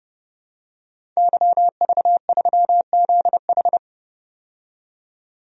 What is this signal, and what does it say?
Morse code, heard as a keyed tone, about 30 wpm, 700 Hz. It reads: YV3ZH
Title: DXCC-100 Call Signs - 5 Characters - 30wpm